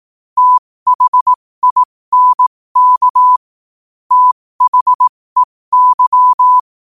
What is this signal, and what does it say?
Morse code, heard as a keyed tone, about 18 wpm, spaced as for 15 wpm, 1000 Hz. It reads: THINK THEY
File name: qotdfiles / 15wpm